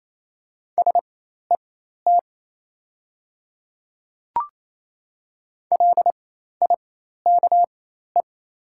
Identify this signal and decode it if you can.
Morse code, heard as a keyed tone, about 28 wpm, spaced as for 13 wpm, 700 Hz. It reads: SET LIKE
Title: 4x Character Spacing - Top 500 Words - 28wpm